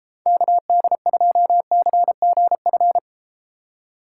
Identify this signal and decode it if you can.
Morse code, heard as a keyed tone, about 33 wpm, 700 Hz. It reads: KD2CGF